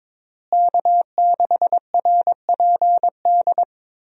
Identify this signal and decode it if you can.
Morse code, heard as a keyed tone, about 22 wpm, 700 Hz. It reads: K6RPD
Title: USA Call Signs - 5 Characters - Repeated 2x - 22wpm